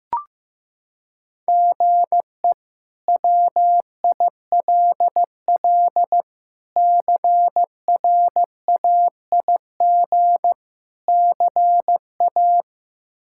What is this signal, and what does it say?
Morse code, heard as a keyed tone, about 15 wpm, 700 Hz. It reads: GE WILL CRAIG CA
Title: K1USN SST QSO Practice - 15wpm